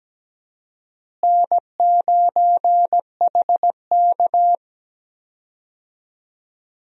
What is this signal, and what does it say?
Morse code, heard as a keyed tone, about 17 wpm, 700 Hz. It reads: N9HK
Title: USA Call Signs - 4 Characters - Repeated 3x - 17wpm